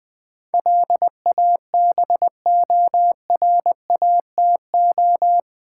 Morse code, heard as a keyed tone, about 20 wpm, 700 Hz. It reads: LABORATO
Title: Build A Word - 4 Letters At A Time - 10-Letter Words - 20wpm